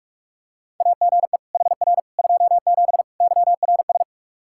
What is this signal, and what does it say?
Morse code, heard as a keyed tone, about 41 wpm, 700 Hz. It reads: AGE HR 27 YRS
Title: Practice QSO - Exchange 3 - More Info - 45wpm